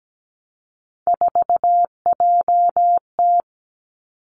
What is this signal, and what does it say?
Morse code, heard as a keyed tone, about 17 wpm, 700 Hz. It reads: 4JT
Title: Three Challenging Letter-Number Combinations 17wpm